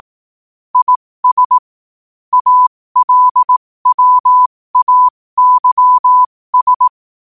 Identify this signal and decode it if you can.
Morse code, heard as a keyed tone, about 18 wpm, spaced as for 15 wpm, 1000 Hz. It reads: IS ALWAYS
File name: qotdfiles / 15wpm